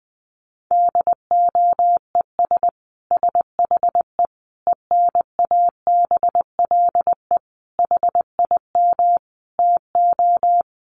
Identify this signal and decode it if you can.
Morse code, heard as a keyed tone, about 20 wpm, 700 Hz. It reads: DOES SHE ENABLE HIM TO
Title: Sentences from Top 1500 Words 20wpm